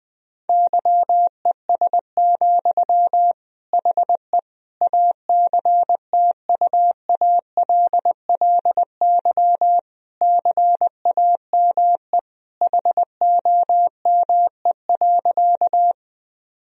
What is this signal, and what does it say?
Morse code, heard as a keyed tone, about 20 wpm, 700 Hz. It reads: YES, HE ACTUALLY CAME HOME.
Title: Sentences from Top 1000 Words - Review Part 4 - 20wpm